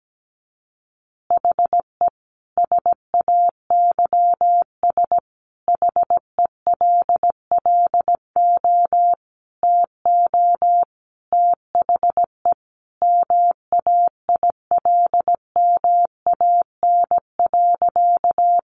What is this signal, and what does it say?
Morse code, heard as a keyed tone, about 17 wpm, 700 Hz. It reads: HE SAYS HELLO TO THE MAILMAN.
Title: Additional Easy Phrases 17wpm